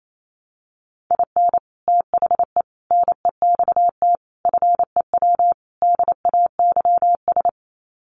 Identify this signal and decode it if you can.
Morse code, heard as a keyed tone, about 28 wpm, 700 Hz. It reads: IN THE NEXT FEW DAYS
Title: Sets of 5 Words Encore2 28wpm